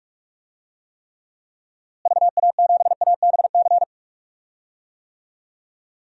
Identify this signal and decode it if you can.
Morse code, heard as a keyed tone, about 46 wpm, 700 Hz. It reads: VA7ABC
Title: DXCC-100 Call Signs - 6 Characters - Repeated 2x - 45wpm